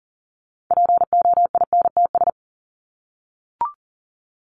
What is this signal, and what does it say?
Morse code, heard as a keyed tone, about 40 wpm, 700 Hz. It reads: POINTS
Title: ES:S Suffix - Top 100 Words with Base Words 40wpm